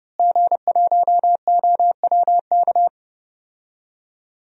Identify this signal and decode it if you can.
Morse code, heard as a keyed tone, about 32 wpm, 700 Hz. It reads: G1OWK
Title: DXCC-100 Call Signs - 5 Characters - Repeated 3x - 30wpm